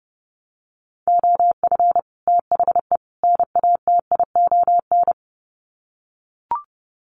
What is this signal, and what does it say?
Morse code, heard as a keyed tone, about 30 wpm, 700 Hz. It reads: OF THE NATION
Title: Sets of 3 Words 30wpm